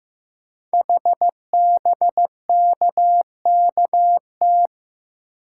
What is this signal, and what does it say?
Morse code, heard as a keyed tone, about 15 wpm, 700 Hz. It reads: HBKKT
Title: Code Groups - 5 Letters - 15wpm